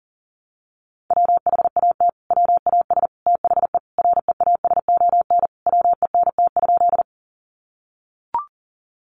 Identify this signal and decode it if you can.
Morse code, heard as a keyed tone, about 38 wpm, 700 Hz. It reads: WHAT WAS THE REASON WENT?
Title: Sentences from Top 700 Words 40wpm